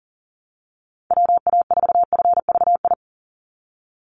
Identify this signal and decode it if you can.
Morse code, heard as a keyed tone, about 40 wpm, 700 Hz. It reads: WA4FVI
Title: USA Call Signs 40wpm